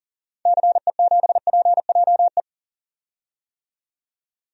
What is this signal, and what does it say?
Morse code, heard as a keyed tone, about 40 wpm, 700 Hz. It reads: CE7PJE
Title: DXCC-100 Call Signs - 6 Characters - 40wpm